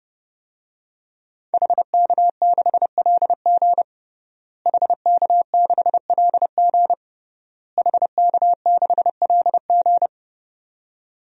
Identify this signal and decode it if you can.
Morse code, heard as a keyed tone, about 31 wpm, 700 Hz. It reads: HK6LG HK6LG HK6LG
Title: DXCC-100 Call Signs - 5 Characters - Repeated 3x - 30wpm